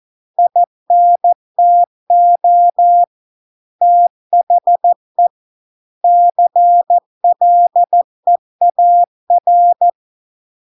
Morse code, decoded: INTO THE CLEAR

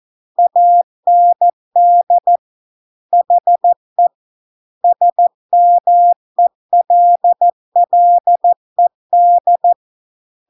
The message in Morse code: AND HE SMELLED

